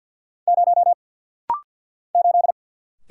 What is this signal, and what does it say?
Morse code, heard as a keyed tone, about 50 wpm, 690 Hz. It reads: 0 8
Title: Single Letter-Number - Rapid-Fire 50wpm